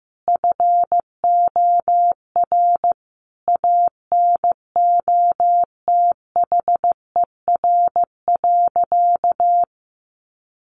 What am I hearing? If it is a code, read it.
Morse code, heard as a keyed tone, about 15 wpm, 700 Hz. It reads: FOR ANOTHER.
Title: Sentences from Top 1400 Words 15wpm